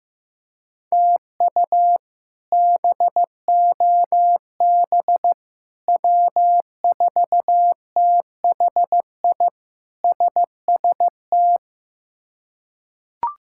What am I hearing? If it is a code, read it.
Morse code, heard as a keyed tone, about 15 wpm, 700 Hz. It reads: TU BOB W4THI SST
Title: K1USN SST QSO Practice - 15wpm